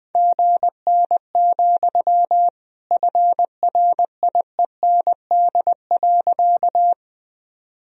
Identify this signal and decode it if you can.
Morse code, heard as a keyed tone, about 20 wpm, 700 Hz. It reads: GN, FRIEND.
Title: Sentences from Top 300 Words and Abbreviations 20wpm